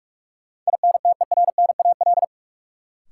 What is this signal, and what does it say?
Morse code, heard as a keyed tone, about 46 wpm, 700 Hz. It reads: INTERNAL
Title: Sentences from Top 1400 Words - New Word Drill - 45wpm